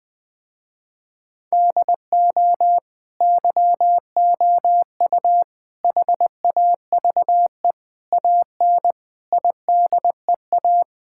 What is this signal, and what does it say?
Morse code, heard as a keyed tone, about 20 wpm, 700 Hz. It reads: DO YOU HAVE AN IDEA
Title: Sentences with 9-Word Phrases - Spoken Beginning - 20wpm